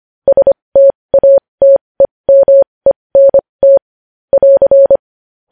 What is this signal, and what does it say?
Morse code, heard as a keyed tone, about 25 wpm, spaced as for 20 wpm, 550 Hz. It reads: STATEMENT <AR>
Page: News Headlines in Morse Code at 20 WPM